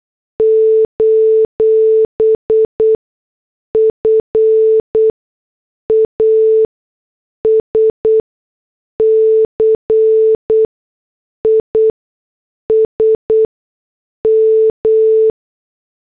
Morse code, heard as a keyed tone, about 8 wpm, spaced as for 6 wpm, 440 Hz. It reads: : FASCISM
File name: morse / headlines